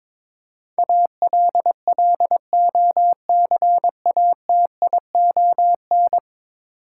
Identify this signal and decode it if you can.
Morse code, heard as a keyed tone, about 21 wpm, 700 Hz. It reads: ALLOCATION
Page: morse-code-ninja-practice-22wpm